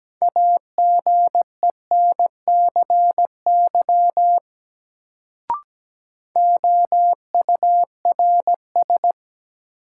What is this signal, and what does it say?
Morse code, heard as a keyed tone, about 17 wpm, 700 Hz. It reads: AGENCY OURS
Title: Sentences from Top 1300 Words - New Word Drill Alt - 17wpm